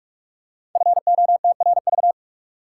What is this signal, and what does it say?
Morse code, heard as a keyed tone, about 44 wpm, 700 Hz. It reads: FOTRV
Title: Code Groups - 5 Letters - 45wpm